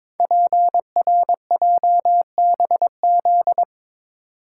Morse code, heard as a keyed tone, about 22 wpm, 700 Hz. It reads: PRJBZ
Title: Code Groups - 5 Letters - 22wpm